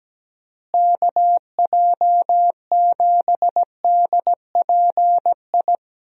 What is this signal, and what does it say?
Morse code, heard as a keyed tone, about 17 wpm, 700 Hz. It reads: KJ7DPI
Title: USA Call Signs - 6 Characters - Repeated 3x - 17wpm